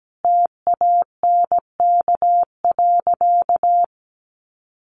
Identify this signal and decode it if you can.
Morse code, heard as a keyed tone, about 17 wpm, 700 Hz. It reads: TANK.
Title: Additional Easy Phrases 17wpm